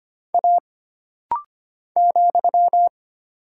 Morse code, heard as a keyed tone, about 25 wpm, 700 Hz. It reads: A ,
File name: Single Character (Letter:Number:Punctuation:Prosign) - Rapid-Fire - 25wpm